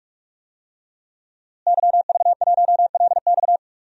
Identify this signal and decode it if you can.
Morse code, heard as a keyed tone, about 42 wpm, 700 Hz. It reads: YV1LX
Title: DXCC-100 Call Signs - 5 Characters - Repeated 2x - 45wpm